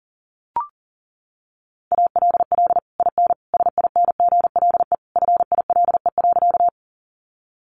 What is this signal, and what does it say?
Morse code, heard as a keyed tone, about 40 wpm, 700 Hz. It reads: ALL IN SINGLE FILE.
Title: Sentences from Top 1200 Words 40wpm